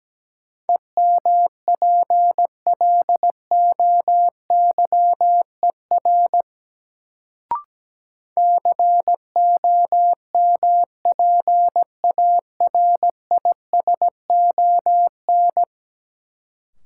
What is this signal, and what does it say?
Morse code, heard as a keyed tone, about 17 wpm, 700 Hz. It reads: EMPLOYER COMPARISON
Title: Sentences from Top 1600 Words - New Word Drill Alt - 17wpm